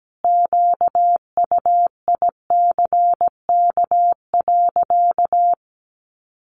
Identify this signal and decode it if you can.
Morse code, heard as a keyed tone, about 17 wpm, 700 Hz. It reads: QUICK.
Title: Sentences from Top 1300 Words 17wpm